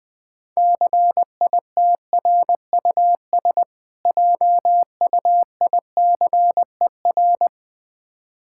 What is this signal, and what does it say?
Morse code, heard as a keyed tone, about 20 wpm, 700 Hz. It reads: CITRUS JUICER
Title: Difficult Words - Speed-Racing with Hints - 20wpm